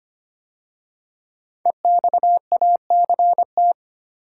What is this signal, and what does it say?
Morse code, heard as a keyed tone, about 25 wpm, 700 Hz. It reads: EXACT
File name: 5 Letter Words - First 3 Letters Given - 25wpm